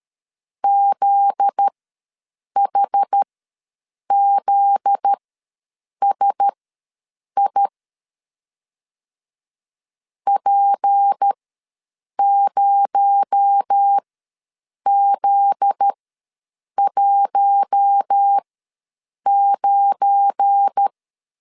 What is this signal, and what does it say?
Morse code, heard as a keyed tone, about 13 wpm, spaced as for 7 wpm, 800 Hz. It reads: ZHZSI P0Z19
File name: 7 GF9Z ws 12wpm